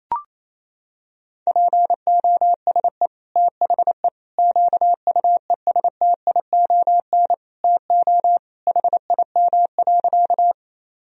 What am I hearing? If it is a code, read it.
Morse code, heard as a keyed tone, about 28 wpm, 700 Hz. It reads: POSE THE QUESTION TO HIM.
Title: Sentences from Top 1000 Words - Review Part 2 - 28wpm